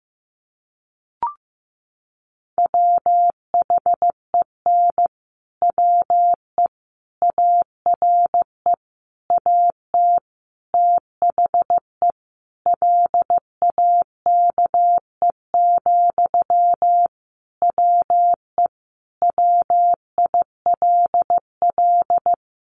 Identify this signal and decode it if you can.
Morse code, heard as a keyed tone, about 15 wpm, 700 Hz. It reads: WHEN WE ARE AT THE LAKE, WE WILL
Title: Sentences from Top 1100 Words 15wpm